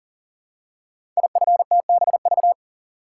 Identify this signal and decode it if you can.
Morse code, heard as a keyed tone, about 41 wpm, 700 Hz. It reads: IFTBV
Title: Code Groups - 5 Letters - 40wpm